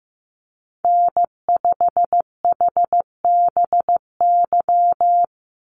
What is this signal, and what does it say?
Morse code, heard as a keyed tone, about 15 wpm, 700 Hz. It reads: N5HBY
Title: USA Call Signs 15wpm